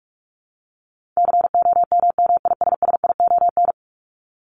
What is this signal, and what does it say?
Morse code, heard as a keyed tone, about 44 wpm, 700 Hz. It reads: COMMISSION
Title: Top 1251-1500 Words 45wpm